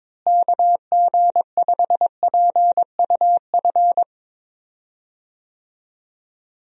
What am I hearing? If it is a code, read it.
Morse code, heard as a keyed tone, about 22 wpm, 700 Hz. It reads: KG5PUF
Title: USA Call Signs - 6 Characters - Repeated 2x - 22wpm